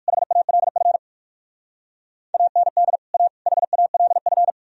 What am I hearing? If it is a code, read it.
Morse code, heard as a keyed tone, about 64 wpm, spaced as for 56 wpm, 700 Hz. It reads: HALF AND A HALF
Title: Speed-Racing - Sets of 3 Words 45wpm